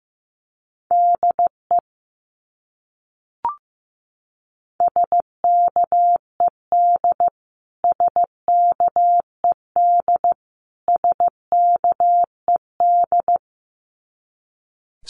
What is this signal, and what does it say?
Morse code, heard as a keyed tone, about 15 wpm, 700 Hz. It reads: DE SKED SKED SKED
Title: Abbreviations Repeated 3x - 15wpm